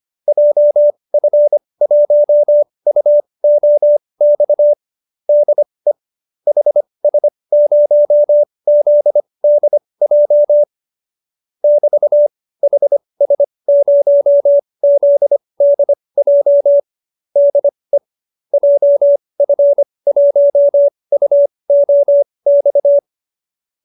Morse code, heard as a keyed tone, about 25 wpm, spaced as for 20 wpm, 590 Hz. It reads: JF1UOX DE HS0ZDJ <BT> HS0ZDJ DE JF1UOX